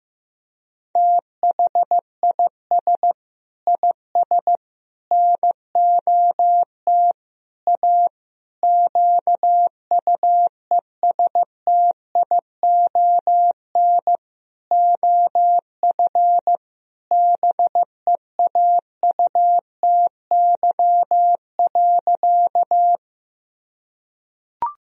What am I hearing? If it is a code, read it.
Morse code, heard as a keyed tone, about 15 wpm, 700 Hz. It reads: THIS IS NOT A QUESTION OF BEAUTY.
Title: Sentences from Top 1000 Words - Review Part 2 - 15wpm